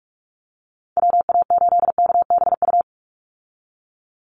Morse code, heard as a keyed tone, about 44 wpm, 700 Hz. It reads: WA8KBU
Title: Call Signs Repeated 3x - 45wpm